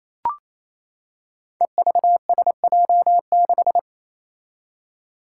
Morse code, heard as a keyed tone, about 27 wpm, 700 Hz. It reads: EVSJ6
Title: Five Challenging Letter-Number Combinations 28wpm